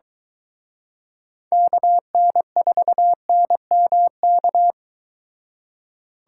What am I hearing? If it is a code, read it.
Morse code, heard as a keyed tone, about 23 wpm, 700 Hz. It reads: KN4NMK